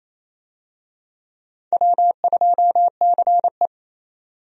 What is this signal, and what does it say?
Morse code, heard as a keyed tone, about 28 wpm, 700 Hz. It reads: W2CE
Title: USA Call Signs - 4 Characters - 28wpm